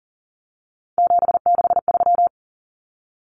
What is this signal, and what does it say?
Morse code, heard as a keyed tone, about 40 wpm, 700 Hz. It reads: 763
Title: Numbers - 3-Digit 40wpm